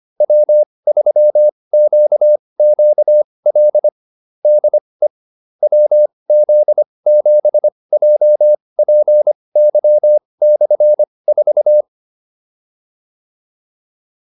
Morse code, heard as a keyed tone, about 25 wpm, spaced as for 20 wpm, 600 Hz. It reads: W3QQL DE WZ7JPY/4
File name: QSO309 20